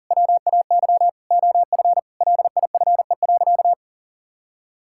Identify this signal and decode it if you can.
Morse code, heard as a keyed tone, about 40 wpm, 700 Hz. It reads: WAY OF LIFE.
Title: Sentences from Top 1000 Words - Review Part 6 - 40wpm